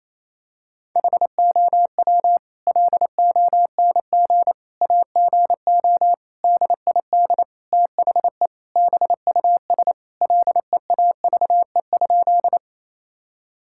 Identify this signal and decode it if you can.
Morse code, heard as a keyed tone, about 28 wpm, 700 Hz. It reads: HOW LONG AGO DID THE BUS LEAVE?
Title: Sentences from Top 1000 Words - Review Part 6 - 28wpm